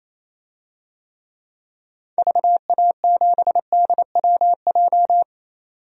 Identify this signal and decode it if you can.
Morse code, heard as a keyed tone, about 28 wpm, 700 Hz. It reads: VA7DWJ
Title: DXCC-100 Call Signs - 6 Characters - 28wpm